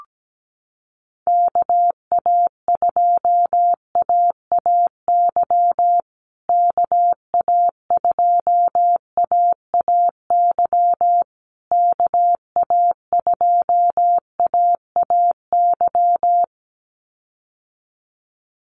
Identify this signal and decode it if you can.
Morse code, heard as a keyed tone, about 17 wpm, 700 Hz. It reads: KA2AAY KA2AAY KA2AAY
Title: USA Call Signs Repeated 3x 17wpm